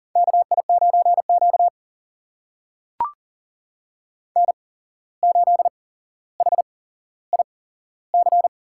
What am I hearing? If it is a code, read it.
Morse code, heard as a keyed tone, about 39 wpm, spaced as for 16 wpm, 700 Hz. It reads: KI9Q N8HIC
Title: USA Call Signs - 8x Character Spacing - 40wpm